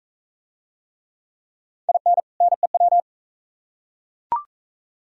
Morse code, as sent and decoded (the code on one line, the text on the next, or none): .. -. / -. .--
IN NEW